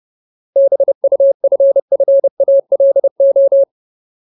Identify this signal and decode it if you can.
Morse code, heard as a keyed tone, about 30 wpm, 550 Hz. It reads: BUFFALO
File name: Ultimate Word List / 30WPM Ultimate